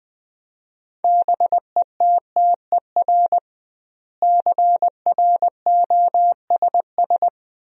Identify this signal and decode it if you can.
Morse code, heard as a keyed tone, about 20 wpm, 700 Hz. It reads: BETTER CROSS
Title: 2x Word-Spacing - Sentences from Top 400 Words - 20wpm